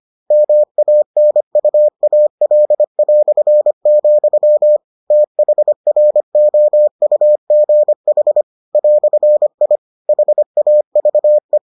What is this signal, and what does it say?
Morse code, heard as a keyed tone, about 25 wpm, 600 Hz. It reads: MANUAL", THROUGH "I HAVE